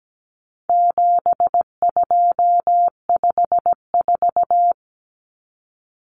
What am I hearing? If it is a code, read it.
Morse code, heard as a keyed tone, about 17 wpm, 700 Hz. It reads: 7254
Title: Numbers - 4-Digit 17wpm